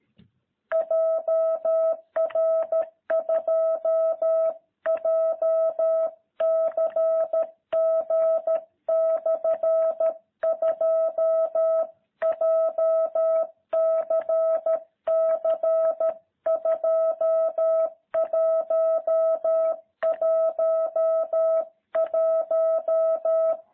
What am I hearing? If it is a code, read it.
Morse code, heard as a keyed tone, about 13 wpm, 660 Hz. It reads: JR2JCG/2 JCC2111
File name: jh8cbh / 2021cw